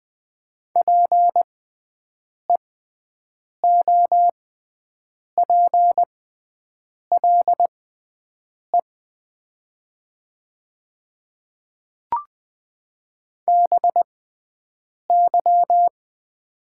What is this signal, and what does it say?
Morse code, heard as a keyed tone, about 20 wpm, spaced as for 7 wpm, 700 Hz. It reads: PEOPLE BY
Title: 6x Character Spacing - Top 300 Words - 20wpm